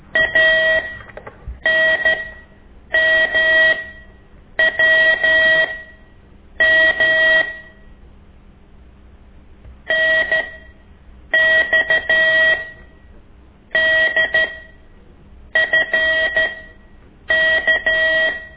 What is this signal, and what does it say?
Morse code, heard as a keyed tone, about 13 wpm, spaced as for 7 wpm, 1900 Hz. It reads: ANMWM NXDFK
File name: ik1weg2 / CORSI CW